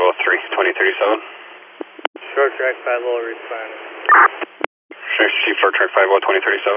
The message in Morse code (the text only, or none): none